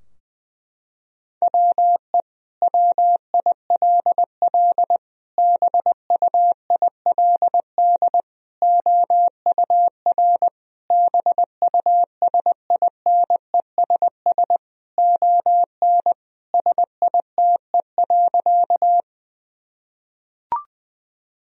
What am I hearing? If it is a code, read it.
Morse code, heard as a keyed tone, about 21 wpm, 700 Hz. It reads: WE WILL BUILD OUR BUSINESS ON SITE.
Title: Sentences from Top 1000 Words - Review Part 4 - 20wpm